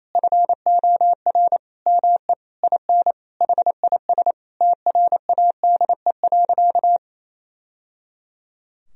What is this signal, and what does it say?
Morse code, heard as a keyed tone, about 28 wpm, 700 Hz. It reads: FOR ME IN HIS TRADE.
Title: Sentences from Top 1000 Words - Review Part 4 - 28wpm